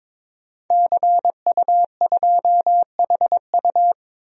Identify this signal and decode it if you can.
Morse code, heard as a keyed tone, about 22 wpm, 700 Hz. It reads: CU2HU
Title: DXCC-100 Call Signs - 5 Characters - 22wpm